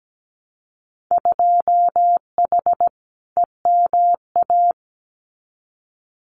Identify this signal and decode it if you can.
Morse code, heard as a keyed tone, about 17 wpm, 700 Hz. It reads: 2H EMA
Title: Winter Field Day Exchanges 17wpm